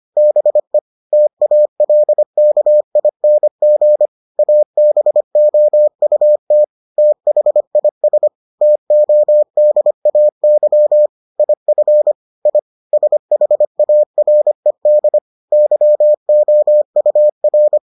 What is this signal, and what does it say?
Morse code, heard as a keyed tone, about 25 wpm, 600 Hz. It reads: BE TALKING ABOUT THIS TODAY IF I SHARED YOUR